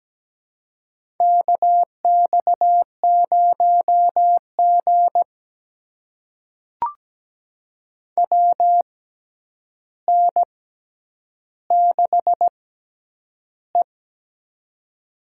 Morse code, heard as a keyed tone, about 17 wpm, spaced as for 9 wpm, 700 Hz. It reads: KX0G WN6E
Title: USA Call Signs - 4 Characters - 6x Character Spacing - 17wpm